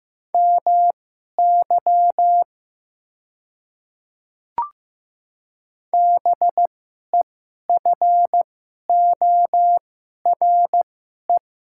Morse code, heard as a keyed tone, about 15 wpm, spaced as for 11 wpm, 700 Hz. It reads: MY BEFORE
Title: 2x Character Spacing - Top 200 Words - 15wpm